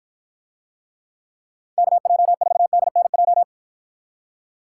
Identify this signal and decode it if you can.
Morse code, heard as a keyed tone, about 52 wpm, 700 Hz. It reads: KJ4DNJ